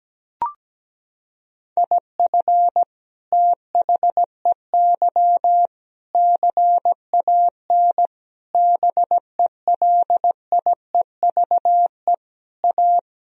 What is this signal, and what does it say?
Morse code, heard as a keyed tone, about 17 wpm, 700 Hz. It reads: IF THEY CAN BELIEVE A
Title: Sentences from Top 1000 Words - Review Part 2 - 17wpm